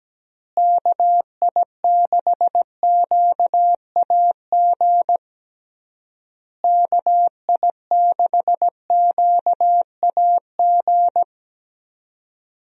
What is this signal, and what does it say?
Morse code, heard as a keyed tone, about 17 wpm, 700 Hz. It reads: KI6QAG KI6QAG
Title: USA Call Signs - 6 Characters - Repeated 3x - 17wpm